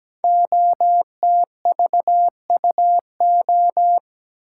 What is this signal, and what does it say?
Morse code, heard as a keyed tone, about 17 wpm, 700 Hz. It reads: OTVUO
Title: Code Groups - 5 Letter:Numbers - 17wpm